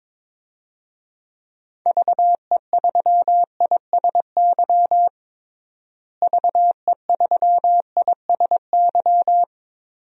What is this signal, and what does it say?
Morse code, heard as a keyed tone, about 22 wpm, 700 Hz. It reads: VE3ISY VE3ISY
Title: DXCC-100 Call Signs - 6 Characters - Repeated 2x - 22wpm